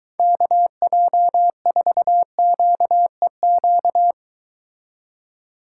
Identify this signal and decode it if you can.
Morse code, heard as a keyed tone, about 23 wpm, 700 Hz. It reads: KJ4QEQ